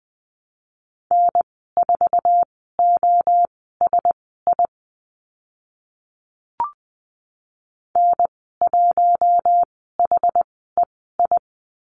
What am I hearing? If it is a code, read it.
Morse code, heard as a keyed tone, about 20 wpm, spaced as for 14 wpm, 700 Hz. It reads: N4OSI N1HEI
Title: 062 Lesson 11g - Letter h - Call Signs 20wpm